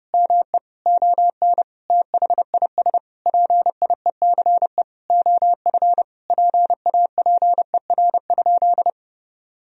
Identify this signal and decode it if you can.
Morse code, heard as a keyed tone, about 30 wpm, 700 Hz. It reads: ME ON THIS PIECE OF PAPER?
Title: Sentences from Top 1000 Words - Review Part 2 - 30wpm